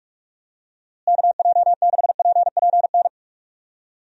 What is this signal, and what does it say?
Morse code, heard as a keyed tone, about 46 wpm, 700 Hz. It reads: KJ6PPN
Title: USA Call Signs - 6 Characters - Repeated 2x - 45wpm